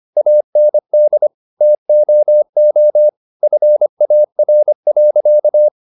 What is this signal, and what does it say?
Morse code, heard as a keyed tone, about 25 wpm, 600 Hz. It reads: AND TOO FAR.
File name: the old man and the sea part 1 25wpm